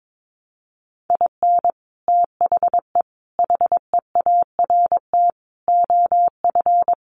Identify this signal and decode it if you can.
Morse code, heard as a keyed tone, about 22 wpm, 700 Hz. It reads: IN THE HEART OF
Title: Sets of 5 Words 22wpm